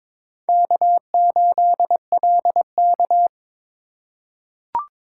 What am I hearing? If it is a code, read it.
Morse code, heard as a keyed tone, about 21 wpm, 700 Hz. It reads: K8LK
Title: USA Call Signs - 4 Characters - 3x Character Spacing - 22wpm